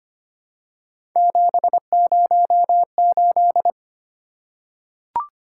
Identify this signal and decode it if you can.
Morse code, heard as a keyed tone, about 25 wpm, 700 Hz. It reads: 708
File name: Three Letter-Number Combinations - 25wpm